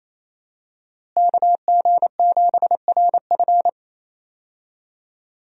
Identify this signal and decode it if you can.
Morse code, heard as a keyed tone, about 27 wpm, 700 Hz. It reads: KG7RF
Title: USA Call Signs - 5 Characters - Repeated 3x - 28wpm